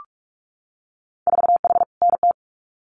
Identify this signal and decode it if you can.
Morse code, heard as a keyed tone, about 46 wpm, 700 Hz. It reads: 4H NT
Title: Winter Field Day Exchanges Encore 45wpm